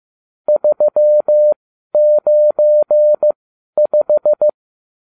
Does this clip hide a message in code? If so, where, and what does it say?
Yes: Morse code from 0.5 to 4.5 s, heard as a keyed tone, about 15 wpm, spaced as for 11 wpm, 600 Hz. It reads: 395